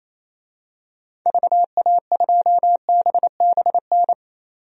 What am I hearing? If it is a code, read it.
Morse code, heard as a keyed tone, about 27 wpm, 700 Hz. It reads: VA2BBN
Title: DXCC-100 Call Signs - 6 Characters - 28wpm